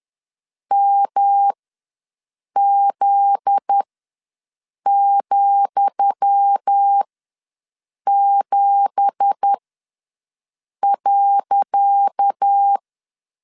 Morse code, heard as a keyed tone, about 10 wpm, spaced as for 6 wpm, 800 Hz. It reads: MZ,7.
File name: LEZIONI / 10 SIGNS / 10D COMA ws 10wpm 4314